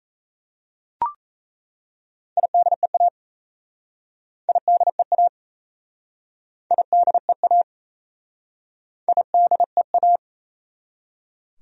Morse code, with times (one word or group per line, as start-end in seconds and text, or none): 2.4-3.1 IDEA
4.5-5.3 IDEA
6.7-7.6 IDEA
9.1-10.2 IDEA